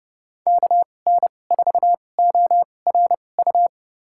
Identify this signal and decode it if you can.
Morse code, heard as a keyed tone, about 30 wpm, spaced as for 22 wpm, 700 Hz. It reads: KN4ORU
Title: USA Call Signs - 6 Characters - 2x Character Spacing - 30wpm